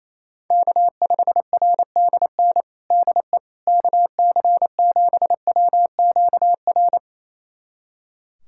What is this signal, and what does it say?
Morse code, heard as a keyed tone, about 28 wpm, 700 Hz. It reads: K5RDN DE KC7WQR
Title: USA Callsign DE Callsign 28wpm